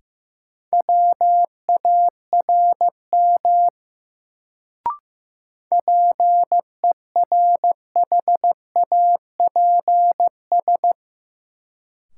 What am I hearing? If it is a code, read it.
Morse code, heard as a keyed tone, about 16 wpm, 700 Hz. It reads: WARM PERHAPS
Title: Sentences from Top 500 Words - New Word Drill - 15wpm